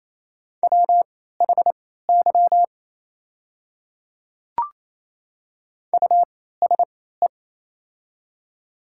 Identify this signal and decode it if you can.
Morse code, heard as a keyed tone, about 27 wpm, spaced as for 16 wpm, 690 Hz. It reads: WHY USE
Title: 3x Character Spacing - Top 200 Words - 28wpm